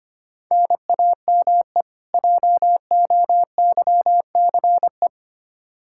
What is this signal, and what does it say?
Morse code, heard as a keyed tone, about 25 wpm, 700 Hz. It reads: NAME JOYCE
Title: K1USN SST QSO Practice 25wpm